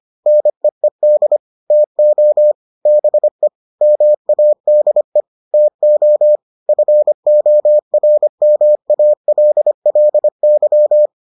Morse code, heard as a keyed tone, about 25 wpm, 600 Hz. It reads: NEED TO BE MADE TO FORMALLY